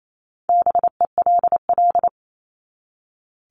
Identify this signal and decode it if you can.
Morse code, heard as a keyed tone, about 28 wpm, 700 Hz. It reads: BELL
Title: Top 1751-2000 Words 28wpm